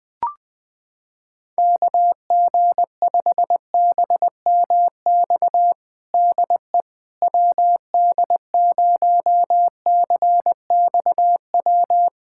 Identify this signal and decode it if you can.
Morse code, heard as a keyed tone, about 20 wpm, 700 Hz. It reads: KG5BMX DE WD0CXW
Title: USA Callsign DE Callsign - 6 Characters - 20wpm